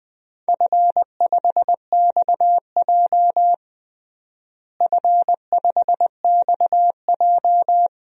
Morse code, heard as a keyed tone, about 21 wpm, 700 Hz. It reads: F5XJ F5XJ
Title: DXCC-100 Call Signs - 4 Characters - Repeated 2x - 20wpm